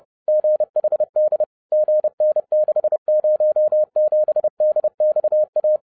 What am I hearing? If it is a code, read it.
Morse code, heard as a keyed tone, about 30 wpm, 610 Hz. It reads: GHD GN607DXA
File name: 30wpm